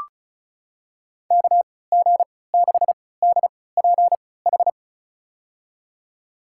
Morse code, heard as a keyed tone, about 36 wpm, spaced as for 20 wpm, 700 Hz. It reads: KG6DPH
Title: USA Call Signs - 3x Character Spacing - 35wpm